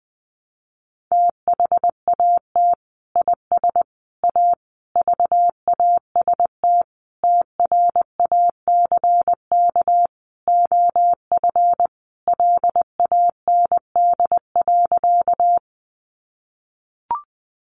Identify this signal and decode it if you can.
Morse code, heard as a keyed tone, about 20 wpm, 700 Hz. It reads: THAT IS A VAST TRACK OF LAND.
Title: Sentences from Top 1300 Words 20wpm